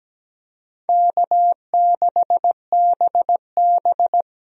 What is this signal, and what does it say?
Morse code, heard as a keyed tone, about 17 wpm, 700 Hz. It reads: K6BB
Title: USA Call Signs - 4 Characters - Repeated 3x - 17wpm